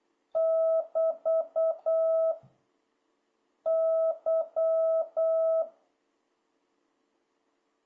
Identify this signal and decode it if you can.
Morse code, heard as a keyed tone, about 8 wpm, 650 Hz. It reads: <BT> Y
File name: jh8cbh / 2026cw